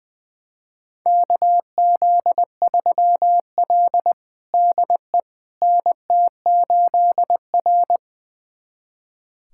Morse code, heard as a keyed tone, about 20 wpm, 700 Hz. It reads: KZ3L DE NT8R
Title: USA Callsign DE Callsign - 4 Characters - 20wpm